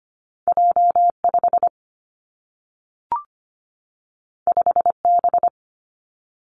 Morse code, heard as a keyed tone, about 25 wpm, 700 Hz. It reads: J 5 5 B
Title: Two Challenging Letter-Number Combinations 25wpm